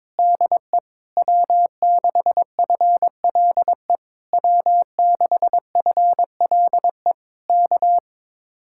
Morse code, heard as a keyed tone, about 21 wpm, 700 Hz. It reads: DE W6FLE W6FLE K
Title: Speed-Racing Common QSO Phrases - 22wpm